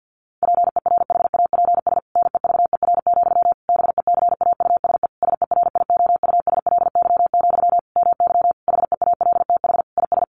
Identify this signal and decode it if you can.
Morse code, heard as a keyed tone, about 50 wpm, 700 Hz. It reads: PERHAPS NEVER, BECAUSE SERIOUSLY, MY HEALTH IS
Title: Sentences from Top 1200 Words 50wpm